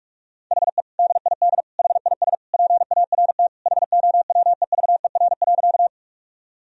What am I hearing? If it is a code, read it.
Morse code, heard as a keyed tone, about 46 wpm, 700 Hz. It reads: HE DID HIS PART HOWEVER.
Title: Sentences from Top 1000 Words - Review Part 1 - 45wpm